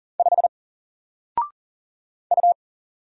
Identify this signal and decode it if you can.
Morse code, heard as a keyed tone, about 40 wpm, 700 Hz. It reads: 5 U
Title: Single Letter-Number 40wpm